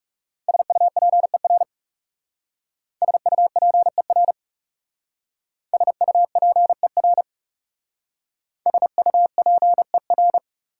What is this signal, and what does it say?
Morse code, heard as a keyed tone, about 39 wpm, 690 Hz. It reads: SUPER SUPER SUPER SUPER